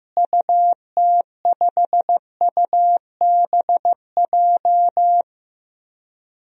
Morse code, heard as a keyed tone, about 15 wpm, 700 Hz. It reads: UT5UBJ
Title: DXCC-100 Call Signs - 6 Characters - Repeated 3x - 15wpm